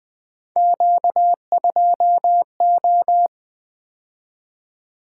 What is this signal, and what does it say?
Morse code, heard as a keyed tone, about 20 wpm, 700 Hz. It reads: Q2O
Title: Three Letter-Number Combinations - 20wpm